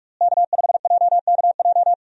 Morse code, heard as a keyed tone, about 45 wpm, 700 Hz. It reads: K5JKJ
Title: USA Call Signs - 5 Characters - Repeated 3x - 45wpm